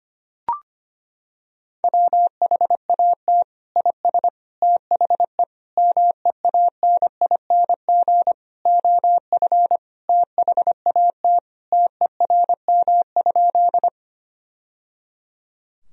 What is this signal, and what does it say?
Morse code, heard as a keyed tone, about 25 wpm, 700 Hz. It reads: WHAT IS THE MEANING OF THAT TERM?
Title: Sentences from Top 1000 Words - Review Part 4 - 25wpm